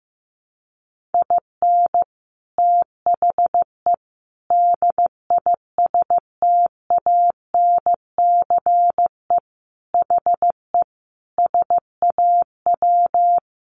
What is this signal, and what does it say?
Morse code, heard as a keyed tone, about 15 wpm, 700 Hz. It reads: IN THE DISTANCE HE SAW
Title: Sentences from Top 1200 Words 15wpm